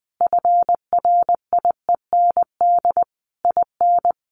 Morse code, heard as a keyed tone, about 20 wpm, 700 Hz. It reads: FRIEND IN
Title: Sentences from Top 1400 Words 20wpm